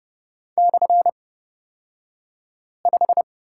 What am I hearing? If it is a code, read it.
Morse code, heard as a keyed tone, about 30 wpm, spaced as for 5 wpm, 700 Hz. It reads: /5
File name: course / 30:10wpm / 235 Lesson 38d - Letter q - Two Characters 30wpm